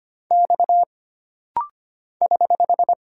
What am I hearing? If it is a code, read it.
Morse code, heard as a keyed tone, about 25 wpm, 700 Hz. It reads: X <HH>
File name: Single Character (Letter:Number:Punctuation:Prosign) - Rapid-Fire - 25wpm